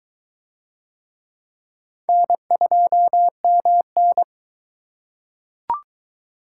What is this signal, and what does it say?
Morse code, heard as a keyed tone, about 22 wpm, 700 Hz. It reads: N2MN